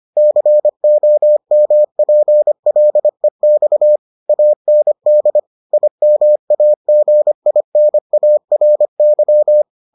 Morse code, heard as a keyed tone, about 25 wpm, 600 Hz. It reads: COMPLEX AND IMAGINARY